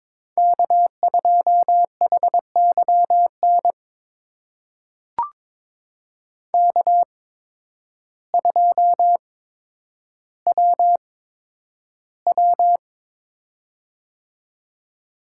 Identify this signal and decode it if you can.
Morse code, heard as a keyed tone, about 22 wpm, spaced as for 10 wpm, 700 Hz. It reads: K2HYN K2WW
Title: USA Call Signs - 8x Character Spacing - 22wpm